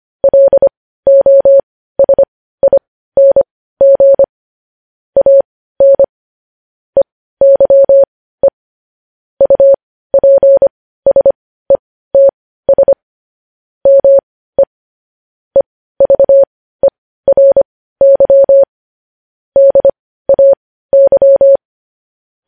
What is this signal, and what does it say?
Morse code, heard as a keyed tone, about 25 wpm, spaced as for 15 wpm, 550 Hz. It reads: LOSING AN EYE UPSETS ME EVERY DAY